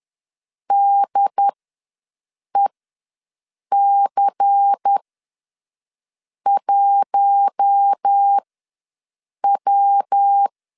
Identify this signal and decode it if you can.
Morse code, heard as a keyed tone, about 10 wpm, spaced as for 5 wpm, 800 Hz. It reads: DEC1W
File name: LEZIONI / 5 LBVC / 5 LBVC ws 10wpm 3733